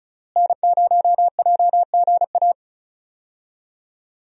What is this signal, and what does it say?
Morse code, heard as a keyed tone, about 36 wpm, 700 Hz. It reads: N0JGA
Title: USA Call Signs - 5 Characters - Repeated 3x - 35wpm